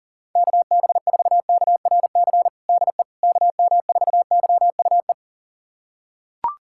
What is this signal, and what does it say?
Morse code, heard as a keyed tone, about 40 wpm, 700 Hz. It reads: KB4KRC DE KM4YUE
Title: USA Callsign DE Callsign - 6 Characters - 40wpm